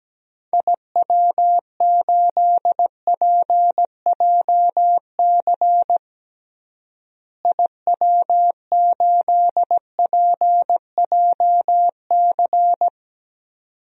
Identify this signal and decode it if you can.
Morse code, heard as a keyed tone, about 17 wpm, 700 Hz. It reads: IW8PJC IW8PJC
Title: DXCC-100 Call Signs - 6 Characters - Repeated 2x - 17wpm